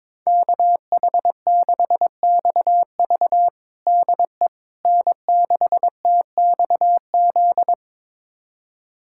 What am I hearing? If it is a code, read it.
Morse code, heard as a keyed tone, about 22 wpm, 700 Hz. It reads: KH6XV DE N6TXZ
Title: USA Callsign DE Callsign - 5 Characters - 22wpm